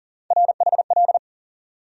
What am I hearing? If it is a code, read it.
Morse code, heard as a keyed tone, about 40 wpm, 700 Hz. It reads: RHL
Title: USA Call Sign Postfixes - 3 Letters 40wpm